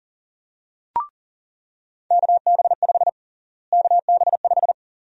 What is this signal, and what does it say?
Morse code, heard as a keyed tone, about 40 wpm, 700 Hz. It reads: KB5 KB5
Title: USA Call Sign Prefixes - Repeated 2x - 40wpm